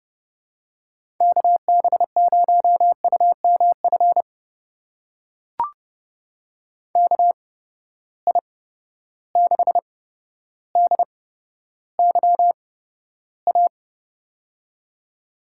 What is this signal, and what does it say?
Morse code, heard as a keyed tone, about 30 wpm, spaced as for 13 wpm, 700 Hz. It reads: KB0UMF KI6DYA
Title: USA Call Signs - 6 Characters - 8x Character Spacing - 30wpm